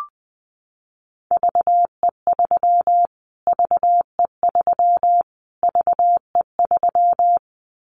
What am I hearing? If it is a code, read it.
Morse code, heard as a keyed tone, about 20 wpm, 700 Hz. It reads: VE3 VE3 VE3
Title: Callsign Prefixes for Top 25 DXCC - Repeated 3x - 20wpm